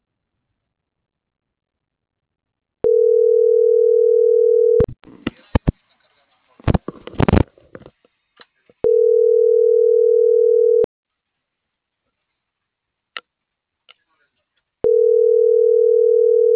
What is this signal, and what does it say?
Ringback tone (North America), 440 + 480 Hz.
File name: CENTRAL TERRESTRE / out claudiacruz - .6923